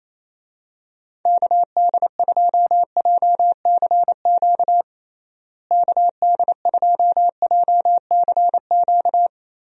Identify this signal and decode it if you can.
Morse code, heard as a keyed tone, about 28 wpm, 700 Hz. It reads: KD2JCQ KD2JCQ
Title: USA Call Signs - 6 Characters - Repeated 3x - 28wpm